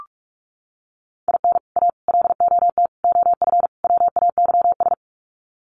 Morse code, heard as a keyed tone, about 44 wpm, 700 Hz. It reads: IN A LOT OF WAYS
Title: Sets of 5 Words Encore 45wpm